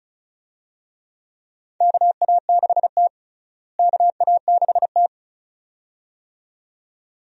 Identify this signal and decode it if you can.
Morse code, heard as a keyed tone, about 36 wpm, 700 Hz. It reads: KA6T KA6T
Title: USA Call Signs - 4 Characters - Repeated 2x - 35wpm